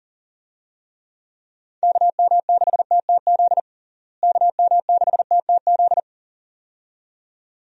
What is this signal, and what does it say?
Morse code, heard as a keyed tone, about 40 wpm, 700 Hz. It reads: KM6TTZ KM6TTZ
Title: USA Call Signs - 6 Characters - Repeated 2x - 40wpm